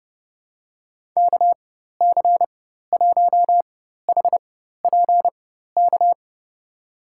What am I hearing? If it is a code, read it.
Morse code, heard as a keyed tone, about 30 wpm, spaced as for 14 wpm, 700 Hz. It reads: KC1HPK
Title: USA Call Signs - 6 Characters - 4x Character Spacing - 30wpm